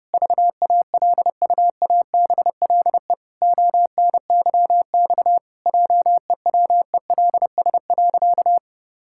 Morse code, heard as a keyed tone, about 30 wpm, 700 Hz. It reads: VALUABLE ONYX JEWELS.
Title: Pangram Challenge v2 - 30wpm